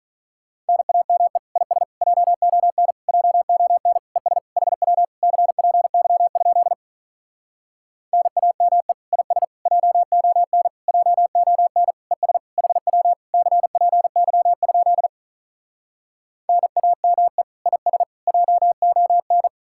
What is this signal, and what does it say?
Morse code, heard as a keyed tone, about 47 wpm, 700 Hz. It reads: NAME IS JON JON ES HW CPY? NAME IS JON JON ES HW CPY? NAME IS JON